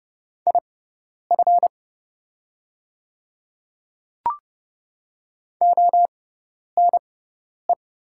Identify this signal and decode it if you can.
Morse code, heard as a keyed tone, about 30 wpm, spaced as for 10 wpm, 690 Hz. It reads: IF ONE
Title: 6x Character Spacing - Top 500 Words - 30wpm